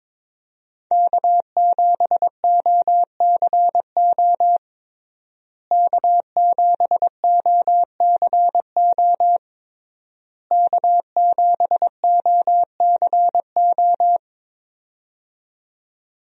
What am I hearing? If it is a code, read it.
Morse code, heard as a keyed tone, about 22 wpm, 700 Hz. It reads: K7OCO K7OCO K7OCO
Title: USA Call Signs - 5 Characters - Repeated 3x - 22wpm